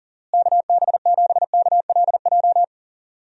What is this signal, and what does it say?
Morse code, heard as a keyed tone, about 40 wpm, 700 Hz. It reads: KB7KLJ
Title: USA Call Signs - 6 Characters - Repeated 2x - 40wpm